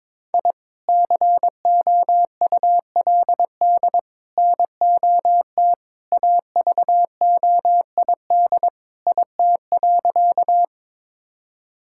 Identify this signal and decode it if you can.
Morse code, heard as a keyed tone, about 22 wpm, 700 Hz. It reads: I COULD NOT AVOID IT.
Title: Sentences from Top 1000 Words - Review Part 6 - 22wpm